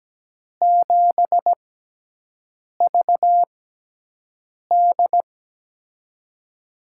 Morse code, heard as a keyed tone, about 17 wpm, spaced as for 6 wpm, 700 Hz. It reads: 7VD
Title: USA Call Signs - 4 Characters - 6x Character Spacing - 17wpm